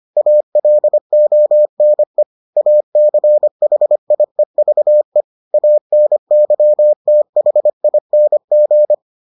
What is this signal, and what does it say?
Morse code, heard as a keyed tone, about 25 wpm, 600 Hz. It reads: ALONE ACHIEVE ANYTHING